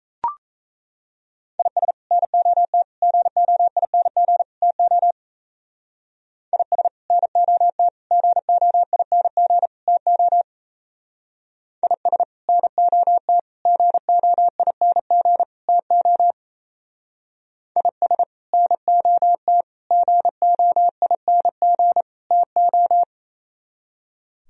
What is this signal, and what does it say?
Morse code, heard as a keyed tone, about 34 wpm, 700 Hz. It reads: IS NOT GOING TO IS NOT GOING TO IS NOT GOING TO IS NOT GOING TO
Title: Speed-Racing - Sets of 4 Words 28wpm